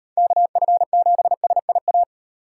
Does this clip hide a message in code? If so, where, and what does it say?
Yes: Morse code from 0.2 to 2.0 s, heard as a keyed tone, about 37 wpm, 700 Hz. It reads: KF7SIA